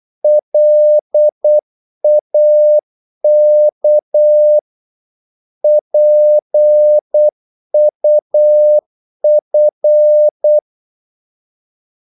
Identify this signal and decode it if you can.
Morse code, heard as a keyed tone, about 8 wpm, 600 Hz. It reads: LAK PUF
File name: Korte ord DK 8wpm